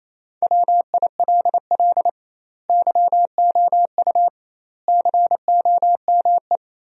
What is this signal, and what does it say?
Morse code, heard as a keyed tone, about 28 wpm, 700 Hz. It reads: WILL YOU COME
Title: 2x Word-Spacing - Sentences from Top 200 Words - 28wpm